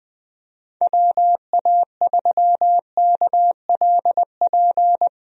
Morse code, heard as a keyed tone, about 20 wpm, 700 Hz. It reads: WA3KLP
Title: USA Call Signs - 6 Characters - Repeated 2x - 20wpm